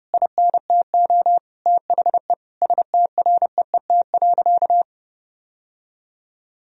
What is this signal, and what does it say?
Morse code, heard as a keyed tone, about 30 wpm, 700 Hz. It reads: INTO THE STREET.
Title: Sentences from Top 1000 Words - Review Part 3 - 30wpm